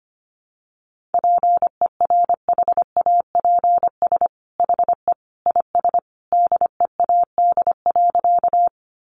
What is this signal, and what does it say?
Morse code, heard as a keyed tone, about 25 wpm, 700 Hz. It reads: PERHAPS HE IS DEAD.
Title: Sentences from Top 1500 Words 25wpm